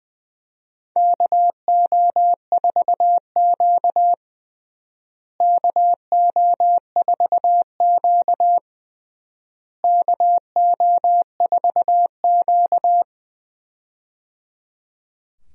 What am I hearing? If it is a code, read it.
Morse code, heard as a keyed tone, about 21 wpm, 700 Hz. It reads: KO4Q KO4Q KO4Q
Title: USA Call Signs - 4 Characters - Repeated 3x - 20wpm